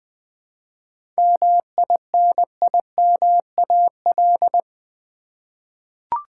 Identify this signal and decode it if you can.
Morse code, heard as a keyed tone, about 20 wpm, 700 Hz. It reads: MINIMAL
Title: 7 Letter Words - First 3 Letters Given - 20wpm